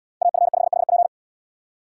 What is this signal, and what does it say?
Morse code, heard as a keyed tone, about 77 wpm, 700 Hz. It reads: AF5HP